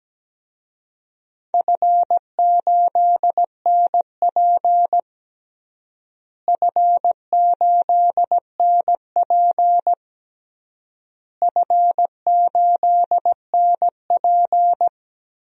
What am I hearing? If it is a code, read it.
Morse code, heard as a keyed tone, about 17 wpm, 700 Hz. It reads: F8NP F8NP F8NP
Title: DXCC-100 Call Signs - 4 Characters - Repeated 3x - 17wpm